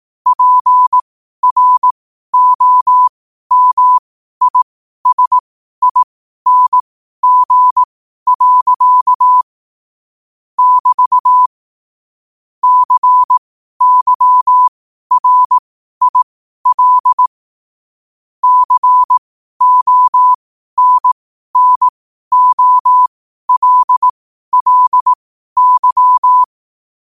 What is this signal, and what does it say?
Morse code, heard as a keyed tone, about 18 wpm, spaced as for 12 wpm, 1000 Hz. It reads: PROMISING. <BT> CYRIL CONNOLLY
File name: qotdfiles / 12wpm